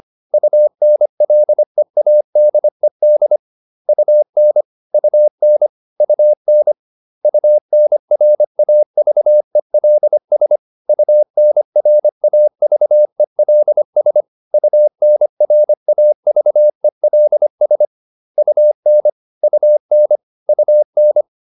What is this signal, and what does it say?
Morse code, heard as a keyed tone, about 25 wpm, 600 Hz. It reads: UNLEADED UN UN UN UNRAVELS UNRAVELS UNRAVELS UN UN UN